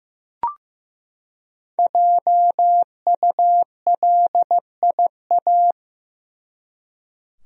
Morse code, heard as a keyed tone, about 15 wpm, 700 Hz. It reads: JULIA
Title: USA Common Names - Max 5 Letters - 15wpm